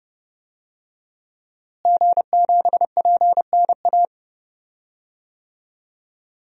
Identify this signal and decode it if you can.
Morse code, heard as a keyed tone, about 30 wpm, 700 Hz. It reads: G7PNA
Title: DXCC-100 Call Signs - 5 Characters - Repeated 3x - 30wpm